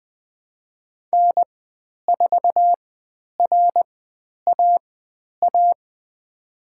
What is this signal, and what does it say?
Morse code, heard as a keyed tone, about 20 wpm, spaced as for 10 wpm, 700 Hz. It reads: N4RAA
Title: 095 Lesson 16g - Letter c - Call Signs 20wpm